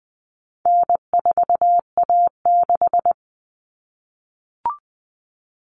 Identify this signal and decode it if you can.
Morse code, heard as a keyed tone, about 20 wpm, 700 Hz. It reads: N4A6
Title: Common Cut Numbers - 4-Digit 20wpm